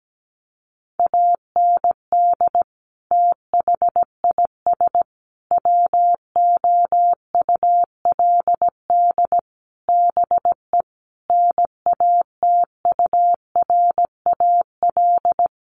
Morse code, heard as a keyed tone, about 17 wpm, 700 Hz. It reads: AND THIS WOULD BE NATURAL
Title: Sentences from Top 1400 Words 17wpm